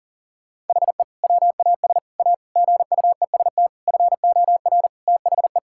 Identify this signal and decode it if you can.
Morse code, heard as a keyed tone, about 40 wpm, 700 Hz. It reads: HE WAS A GUEST FOR THE
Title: Sentences from Top 1600 Words 40wpm